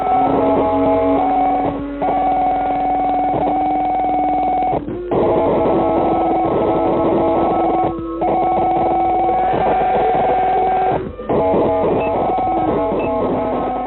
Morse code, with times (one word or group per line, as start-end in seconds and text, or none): none